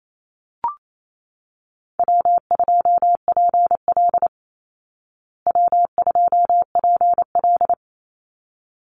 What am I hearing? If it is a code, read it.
Morse code, heard as a keyed tone, about 28 wpm, 700 Hz. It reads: W2PL W2PL
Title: Easy Call Signs Repeated 2x - 28wpm